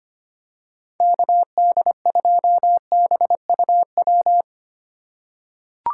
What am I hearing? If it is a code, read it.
Morse code, heard as a keyed tone, about 25 wpm, 700 Hz. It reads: KD2BUW
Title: USA Call Signs - 6 Characters - 4x Character Spacing - 25wpm